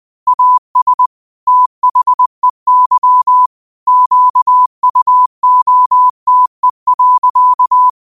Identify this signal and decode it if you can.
Morse code, heard as a keyed tone, about 20 wpm, 1000 Hz. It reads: AS THEY QUOTE.